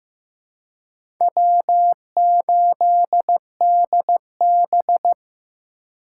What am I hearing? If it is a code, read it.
Morse code, heard as a keyed tone, about 15 wpm, 700 Hz. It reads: W8DB
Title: USA Call Signs - 4 Characters - 2x Character Spacing - 15wpm